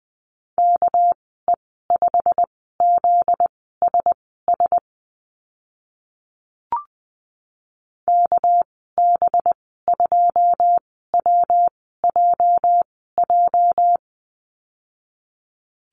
Morse code, heard as a keyed tone, about 20 wpm, spaced as for 14 wpm, 700 Hz. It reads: KE5ZSS KB2WJJ
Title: 245 Lesson 39g - Letter z - Call Signs 20wpm